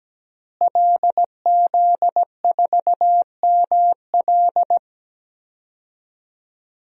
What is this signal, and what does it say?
Morse code, heard as a keyed tone, about 17 wpm, 700 Hz. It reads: LZ4ML
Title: DXCC-100 Call Signs - 5 Characters - Repeated 2x - 17wpm